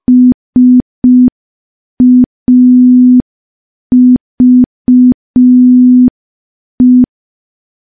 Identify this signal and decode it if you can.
Morse code, heard as a keyed tone, about 5 wpm, 250 Hz. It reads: SAVE
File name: morse / morse-training-05WPM